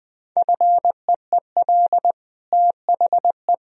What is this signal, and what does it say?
Morse code, heard as a keyed tone, about 20 wpm, 700 Hz. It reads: FEEL THE
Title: Sentences from Top 1000 Words - Review Part 2 - 20wpm